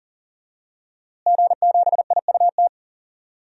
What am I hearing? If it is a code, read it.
Morse code, heard as a keyed tone, about 40 wpm, 700 Hz. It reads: G7IUT
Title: DXCC-100 Call Signs - 5 Characters - Repeated 2x - 40wpm